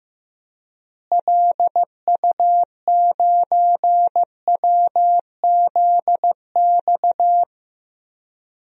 Morse code, heard as a keyed tone, about 15 wpm, 700 Hz. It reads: LU9WZX
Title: DXCC-100 Call Signs - 6 Characters - Repeated 3x - 15wpm